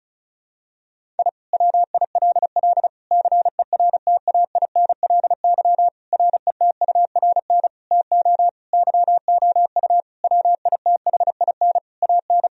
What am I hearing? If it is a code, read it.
Morse code, heard as a keyed tone, about 36 wpm, 700 Hz. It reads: I WILL CERTAINLY RETURN TO YOU WITHIN AN
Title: Sentences from Top 1000 Words - Review Part 4 - 35wpm